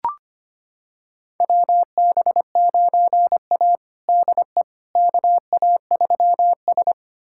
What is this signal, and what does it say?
Morse code, heard as a keyed tone, about 25 wpm, 700 Hz. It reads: WB9A DE KA3S
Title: USA Callsign DE Callsign - 4 Characters - 25wpm